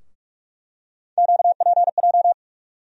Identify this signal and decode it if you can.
Morse code, heard as a keyed tone, about 46 wpm, 700 Hz. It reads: QPJ
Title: USA Call Sign Postfixes - 3 Letters 45wpm